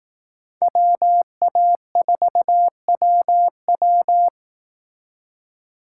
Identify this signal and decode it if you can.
Morse code, heard as a keyed tone, about 18 wpm, 700 Hz. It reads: WA4WW
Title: Speed Racing - USA 2x2 Call Signs 15wpm